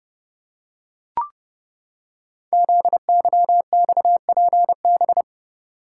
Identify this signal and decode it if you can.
Morse code, heard as a keyed tone, about 30 wpm, 700 Hz. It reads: ZYXPB
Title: Code Groups - 5 Letter:Numbers - 30wpm